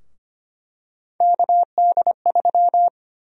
Morse code, heard as a keyed tone, about 25 wpm, 700 Hz. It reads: KD3
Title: USA Call Sign Prefixes - 25wpm